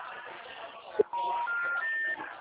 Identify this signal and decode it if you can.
Special information tone, with speech.
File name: out- -2003- - - .7892